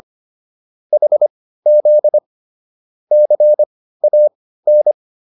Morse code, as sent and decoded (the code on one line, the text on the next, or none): .... --.. / -.-. .- -.
HZ CAN